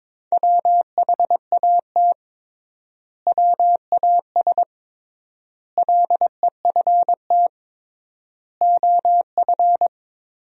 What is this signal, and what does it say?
Morse code, heard as a keyed tone, about 22 wpm, 700 Hz. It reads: WHAT WAS LEFT OF
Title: 3x Word-Spacing - Sentences from Top 300 Words - 22wpm